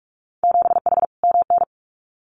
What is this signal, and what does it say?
Morse code, heard as a keyed tone, about 44 wpm, 690 Hz. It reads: 7H MN
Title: Winter Field Day Exchanges Encore 45wpm